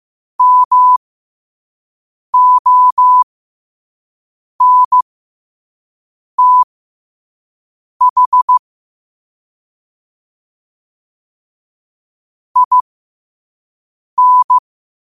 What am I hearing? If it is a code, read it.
Morse code, heard as a keyed tone, about 15 wpm, spaced as for 5 wpm, 1000 Hz. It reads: MONTH IN